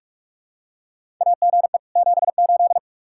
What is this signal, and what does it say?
Morse code, heard as a keyed tone, about 47 wpm, 700 Hz. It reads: AGE 78
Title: Practice QSO - Exchange 3 - More Info - 45wpm